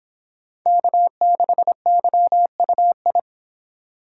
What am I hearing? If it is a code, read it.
Morse code, heard as a keyed tone, about 25 wpm, 700 Hz. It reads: K6YUI